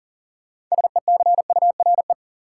Morse code, heard as a keyed tone, about 40 wpm, 700 Hz. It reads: SECURE
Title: Sentences from Top 1200 Words - New Word Drill - 40wpm